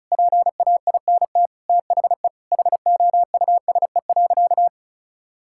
Morse code, heard as a keyed tone, about 34 wpm, 700 Hz. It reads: PAINT THE HOUSE.
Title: Sentences from Top 1000 Words - Review Part 6 - 35wpm